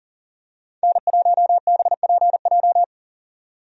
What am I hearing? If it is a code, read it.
Morse code, heard as a keyed tone, about 40 wpm, 700 Hz. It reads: N1BPJ
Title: USA Call Signs - 5 Characters - 40wpm